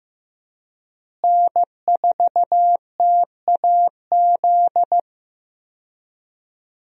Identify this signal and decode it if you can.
Morse code, heard as a keyed tone, about 15 wpm, 700 Hz. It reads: N4TAZ
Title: USA Call Signs - 5 Characters - Repeated 2x - 15wpm